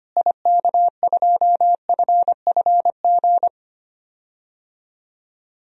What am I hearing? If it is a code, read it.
Morse code, heard as a keyed tone, about 25 wpm, 700 Hz. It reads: IK2FFG
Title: DXCC-100 Call Signs - 6 Characters - Repeated 2x - 25wpm